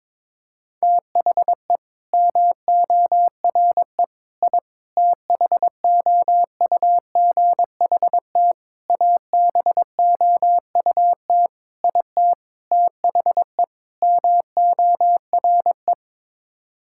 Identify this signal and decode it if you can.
Morse code, heard as a keyed tone, about 22 wpm, 700 Hz. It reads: THE MORE I THOUGHT ABOUT IT THE MORE
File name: practice / 22wpm / Sets of 8 Words - 22wpm